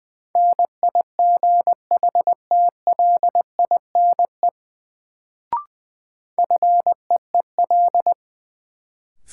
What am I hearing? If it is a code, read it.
Morse code, heard as a keyed tone, about 20 wpm, 700 Hz. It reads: NIGHTLINE FEEL
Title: Sets of 5 Words - New Word Drill - 20wpm